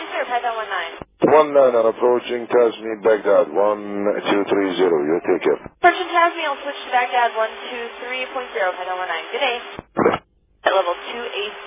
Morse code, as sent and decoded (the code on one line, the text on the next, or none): none